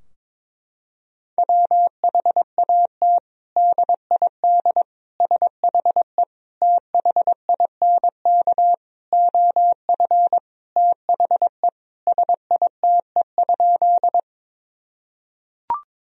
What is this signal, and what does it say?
Morse code, heard as a keyed tone, about 21 wpm, 700 Hz. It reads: WHAT DID SHE THINK OF THE SITE?
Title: Sentences from Top 1000 Words - Review Part 4 - 22wpm